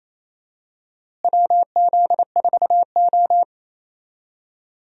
Morse code, heard as a keyed tone, about 28 wpm, 700 Hz. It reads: WZ4O
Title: USA Call Signs - 4 Characters - Repeated 2x - 28wpm